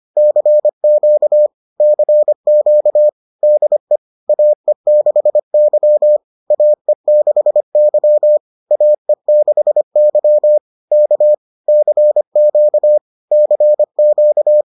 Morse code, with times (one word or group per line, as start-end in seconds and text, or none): 0.2-1.5 CQ
1.8-3.1 CQ
3.4-4.0 DE
4.3-6.2 AE6Y
6.5-8.4 AE6Y
8.7-10.6 AE6Y
10.9-11.3 K
11.7-13.0 CQ
13.3-14.6 CQ